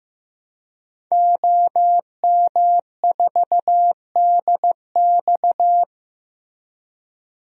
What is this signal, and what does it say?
Morse code, heard as a keyed tone, about 15 wpm, 700 Hz. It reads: OM4DX
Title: DXCC-100 Call Signs - 5 Characters - 15wpm